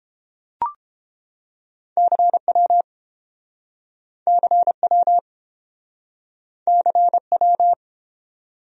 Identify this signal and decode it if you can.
Morse code, heard as a keyed tone, about 28 wpm, 700 Hz. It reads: CW CW CW